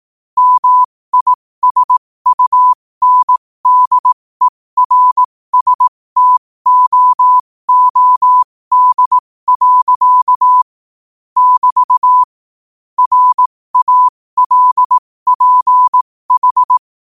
Morse code, heard as a keyed tone, about 18 wpm, spaced as for 15 wpm, 1000 Hz. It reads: MISUNDERSTOOD. <BT> RALPH
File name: qotdfiles / 15wpm